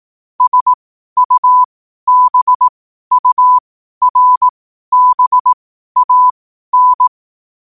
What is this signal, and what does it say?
Morse code, heard as a keyed tone, about 18 wpm, spaced as for 12 wpm, 1000 Hz. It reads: SUBURBAN
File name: qotdfiles / 12wpm